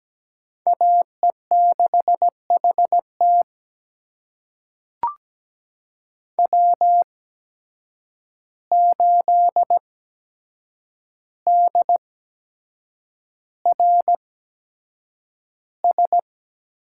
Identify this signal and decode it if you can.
Morse code, heard as a keyed tone, about 17 wpm, spaced as for 7 wpm, 700 Hz. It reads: AE6HT W8DRS
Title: USA Call Signs - 5 Characters - 8x Character Spacing - 17wpm